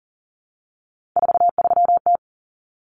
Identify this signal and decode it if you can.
Morse code, heard as a keyed tone, about 40 wpm, 700 Hz. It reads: 43T
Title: Common Cut Numbers - 3-Digit 40wpm